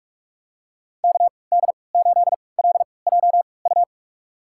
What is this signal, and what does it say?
Morse code, heard as a keyed tone, about 43 wpm, spaced as for 25 wpm, 690 Hz. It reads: KD8LJU
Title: USA Call Signs - 6 Characters - 3x Character Spacing - 45wpm